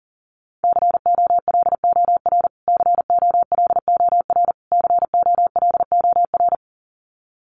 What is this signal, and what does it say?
Morse code, heard as a keyed tone, about 40 wpm, 700 Hz. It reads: COLOR COLOR COLOR
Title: Top 500 Words Repeated 40wpm